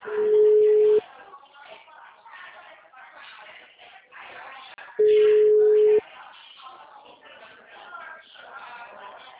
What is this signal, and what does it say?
Ringback tone (Europe, 425 Hz), with speech.